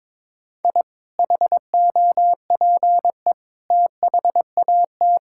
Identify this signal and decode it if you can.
Morse code, heard as a keyed tone, about 22 wpm, 700 Hz. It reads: I HOPE THAT
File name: practice / 22wpm / Sentences with 3-Word Phrases - 22wpm